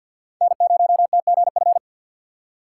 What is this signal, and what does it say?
Morse code, heard as a keyed tone, about 52 wpm, 700 Hz. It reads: N0TGF
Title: USA Call Signs - 5 Characters - 3x Character Spacing - 50wpm